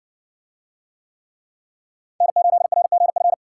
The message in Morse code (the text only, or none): N8RGF